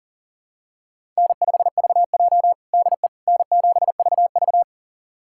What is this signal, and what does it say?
Morse code, heard as a keyed tone, about 40 wpm, 700 Hz. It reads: N5VJ DE N7VV
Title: USA Callsign DE Callsign - 4 Characters - 40wpm